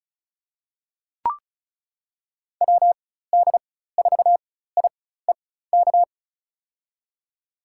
Morse code, heard as a keyed tone, about 36 wpm, spaced as for 16 wpm, 700 Hz. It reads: WD4IEK
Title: USA Call Signs - 4x Character Spacing - 35wpm